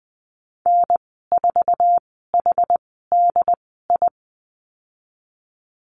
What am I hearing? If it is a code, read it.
Morse code, heard as a keyed tone, about 19 wpm, spaced as for 14 wpm, 700 Hz. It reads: N4HDI
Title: 179 Lesson 29g - Slash - Call Signs 20wpm